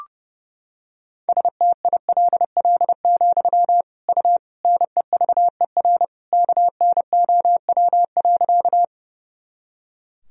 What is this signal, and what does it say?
Morse code, heard as a keyed tone, about 30 wpm, 700 Hz. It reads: STILL, U NEVER KNOW.
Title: Sentences from Top 300 Words and Abbreviations 30wpm